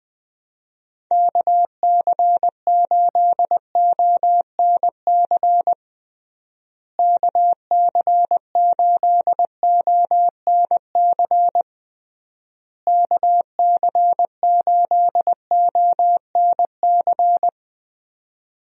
Morse code, heard as a keyed tone, about 20 wpm, 700 Hz. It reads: KC8ONC KC8ONC KC8ONC
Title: USA Call Signs - 6 Characters - Repeated 3x - 20wpm